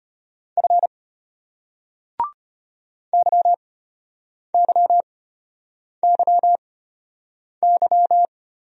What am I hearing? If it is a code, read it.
Morse code, heard as a keyed tone, about 32 wpm, spaced as for 8 wpm, 700 Hz. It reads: F YYYY